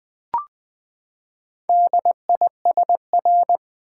Morse code, heard as a keyed tone, about 20 wpm, 700 Hz. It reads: DISR
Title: Build A Word - 4 Letters At A Time - 10-Letter Words - 20wpm